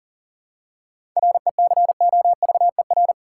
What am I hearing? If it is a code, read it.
Morse code, heard as a keyed tone, about 40 wpm, 700 Hz. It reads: RECOVER
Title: Seven Letter Words - 40wpm